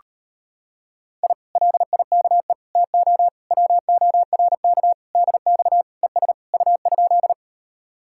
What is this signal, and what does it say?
Morse code, heard as a keyed tone, about 33 wpm, 700 Hz. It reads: I LIKE TO WORK DX ES U?